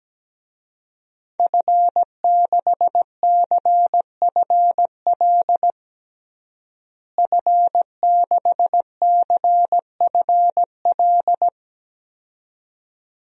Morse code, heard as a keyed tone, about 17 wpm, 700 Hz. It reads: F6CFL F6CFL
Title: DXCC-100 Call Signs - 5 Characters - Repeated 2x - 17wpm